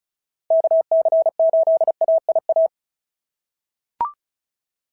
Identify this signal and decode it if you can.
Morse code, heard as a keyed tone, about 36 wpm, 690 Hz. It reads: KC8AIA
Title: SOTA QSO Practice 35wpm